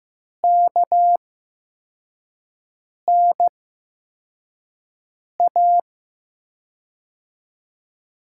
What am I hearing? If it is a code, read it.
Morse code, heard as a keyed tone, about 15 wpm, spaced as for 4 wpm, 700 Hz. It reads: KNA
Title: USA Call Signs - 6 Characters - 8x Character Spacing - 15wpm